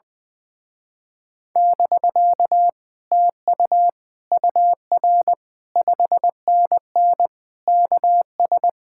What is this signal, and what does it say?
Morse code, heard as a keyed tone, about 20 wpm, 700 Hz. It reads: <BK> TU UR 5NN KS
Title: Speed-Racing Common QSO Phrases - 20wpm